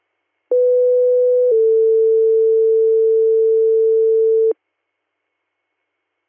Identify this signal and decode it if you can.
Two-tone page, heard as a one-second tone, then a three-second tone, about 500 then 450 Hz. It is heard from 0.5 to 4.5 s.